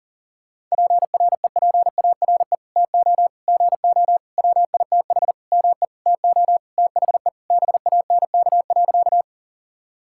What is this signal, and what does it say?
Morse code, heard as a keyed tone, about 40 wpm, 700 Hz. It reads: PREPARE TO GO WITH ME TO THE BANK.
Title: Sentences from Top 1000 Words - Review Part 6 - 40wpm